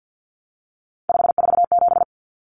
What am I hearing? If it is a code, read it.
Morse code, heard as a keyed tone, about 50 wpm, 700 Hz. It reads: 547
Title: Numbers - 3-Digit 50wpm